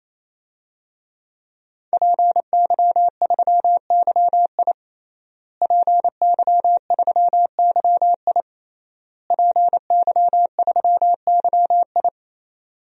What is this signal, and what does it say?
Morse code, heard as a keyed tone, about 28 wpm, 700 Hz. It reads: PY3YI PY3YI PY3YI
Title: DXCC-100 Call Signs - 5 Characters - Repeated 3x - 28wpm